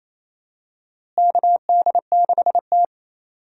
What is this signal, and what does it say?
Morse code, heard as a keyed tone, about 28 wpm, 700 Hz. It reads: KD6T
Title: USA Call Signs - 4 Characters - Repeated 2x - 28wpm